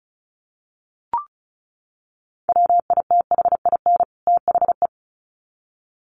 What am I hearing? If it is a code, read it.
Morse code, heard as a keyed tone, about 34 wpm, 700 Hz. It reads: WITHIN THE
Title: Sets of 2 Words 35wpm